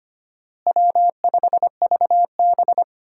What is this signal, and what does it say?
Morse code, heard as a keyed tone, about 25 wpm, 700 Hz. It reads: W5VB
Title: USA Call Signs - 4 Characters - Repeated 2x - 25wpm